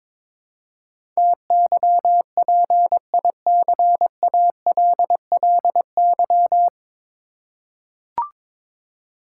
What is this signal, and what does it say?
Morse code, heard as a keyed tone, about 21 wpm, 700 Hz. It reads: TYPICALLY
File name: practice / 22wpm / Nine Letter Words - 22wpm